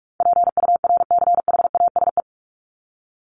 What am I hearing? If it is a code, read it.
Morse code, heard as a keyed tone, about 46 wpm, 700 Hz. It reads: PURCHASE
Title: Top 501-750 Words 45wpm